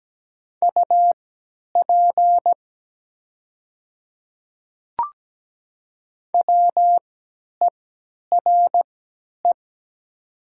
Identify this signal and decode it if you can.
Morse code, heard as a keyed tone, about 17 wpm, spaced as for 10 wpm, 700 Hz. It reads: UP WERE
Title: 3x Character Spacing - Top 100 Words - 17wpm